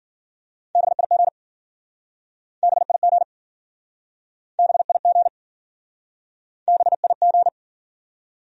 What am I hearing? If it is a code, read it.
Morse code, heard as a keyed tone, about 52 wpm, 700 Hz. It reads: BIG BIG BIG BIG